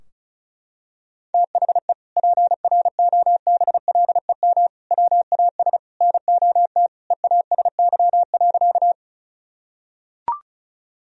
Morse code, heard as a keyed tone, about 36 wpm, 700 Hz. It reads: THE PROBLEM WAS NOT EASY.
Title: Sentences from Top 1000 Words - Review Part 2 - 35wpm